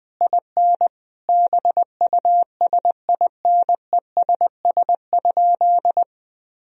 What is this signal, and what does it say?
Morse code, heard as a keyed tone, about 20 wpm, 700 Hz. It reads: IN BUSINESS?
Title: Sentences from Top 1000 Words - Review Part 3 - 20wpm